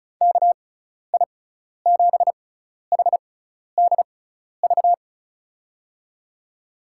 Morse code, heard as a keyed tone, about 36 wpm, spaced as for 12 wpm, 700 Hz. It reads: KI7HDV
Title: USA Call Signs - 6 Characters - 6x Character Spacing - 35wpm